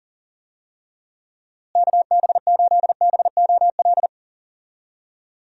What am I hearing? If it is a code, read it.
Morse code, heard as a keyed tone, about 40 wpm, 700 Hz. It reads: KB8BOL
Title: USA Call Signs - 6 Characters - Repeated 2x - 40wpm